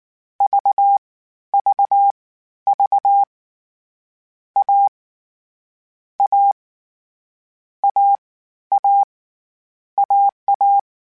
Morse code, heard as a keyed tone, about 19 wpm, spaced as for 12 wpm, 800 Hz. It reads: VVV A A AA AA
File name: Difficult Combos / Difficult Combos-mp3s / AWJ ShortRun@18